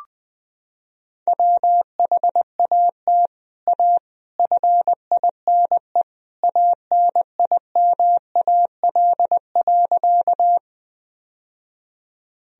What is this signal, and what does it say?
Morse code, heard as a keyed tone, about 20 wpm, 700 Hz. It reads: WHAT A FINE ANIMAL.
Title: Sentences from Top 1000 Words - Review Part 1 - 20wpm